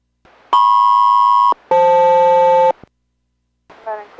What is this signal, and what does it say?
Aircraft SELCAL, heard as two pairs of tones, with speech. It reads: MP-FL